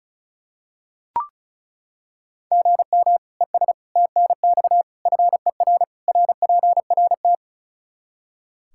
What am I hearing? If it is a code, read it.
Morse code, heard as a keyed tone, about 36 wpm, 700 Hz. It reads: GM ES TNX FER RPRT
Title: Practice QSO - Exchange 1 - Beginning - 35wpm